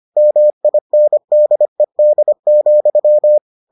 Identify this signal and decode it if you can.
Morse code, heard as a keyed tone, about 25 wpm, 600 Hz. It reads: MINDED,